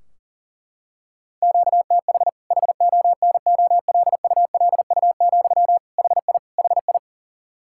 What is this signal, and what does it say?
Morse code, heard as a keyed tone, about 37 wpm, 700 Hz. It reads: QTH HONOLULU, HI HI
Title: Practice QSO - Exchange 1 - Beginning - 40wpm